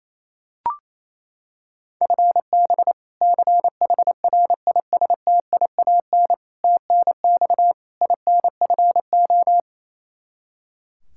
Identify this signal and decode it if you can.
Morse code, heard as a keyed tone, about 27 wpm, 700 Hz. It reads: FB CHRISTIAN TNX INFO
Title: K1USN SST QSO Practice 28wpm